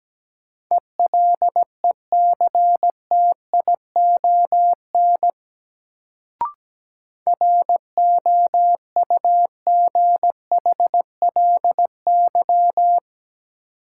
Morse code, heard as a keyed tone, about 17 wpm, 700 Hz. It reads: ELECTION ROUGHLY
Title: Sentences from Top 1500 Words - New Word Drill - 17wpm